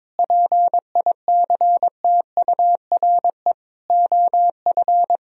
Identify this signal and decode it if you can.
Morse code, heard as a keyed tone, about 22 wpm, 700 Hz. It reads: PICTURE OF
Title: Sentences from Top 1000 Words - Review Part 1 - 22wpm